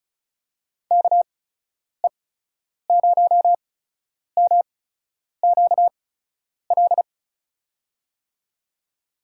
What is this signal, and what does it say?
Morse code, heard as a keyed tone, about 34 wpm, spaced as for 10 wpm, 700 Hz. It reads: KE0MQL
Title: USA Call Signs - 8x Character Spacing - 35wpm